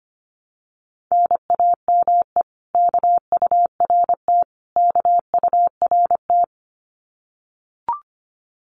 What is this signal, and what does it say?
Morse code, heard as a keyed tone, about 25 wpm, 700 Hz. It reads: NAME KURT KURT
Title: Common QSO Phrases Encore 25wpm